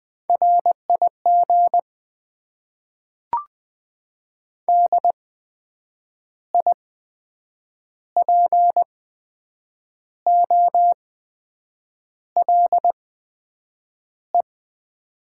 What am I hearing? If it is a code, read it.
Morse code, heard as a keyed tone, about 20 wpm, spaced as for 7 wpm, 700 Hz. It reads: RIG DIPOLE
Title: Top 90 QSO Elements - 8x Character Spacing - 20wpm